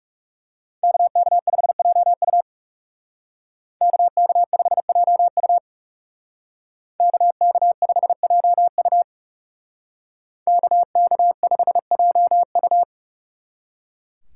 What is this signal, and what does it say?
Morse code, heard as a keyed tone, about 39 wpm, 700 Hz. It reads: KK5JU KK5JU KK5JU KK5JU